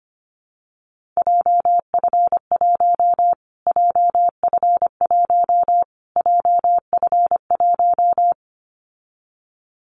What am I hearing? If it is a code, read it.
Morse code, heard as a keyed tone, about 25 wpm, 700 Hz. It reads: JF1 JF1 JF1
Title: Callsign Prefixes for Top 25 DXCC - Repeated 3x - 25wpm